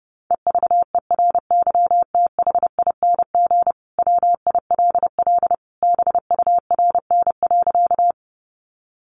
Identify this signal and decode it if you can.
Morse code, heard as a keyed tone, about 29 wpm, 700 Hz. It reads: EVERYTHING WILL BURN.
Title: Sentences from Top 1300 Words 30wpm